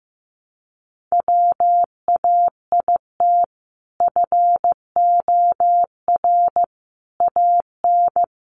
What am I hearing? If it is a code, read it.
Morse code, heard as a keyed tone, about 15 wpm, 700 Hz. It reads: WAIT FOR AN
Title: Sentences from Top 1000 Words 15wpm